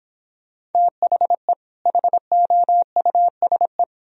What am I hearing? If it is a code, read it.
Morse code, heard as a keyed tone, about 29 wpm, 700 Hz. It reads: THE HOUSE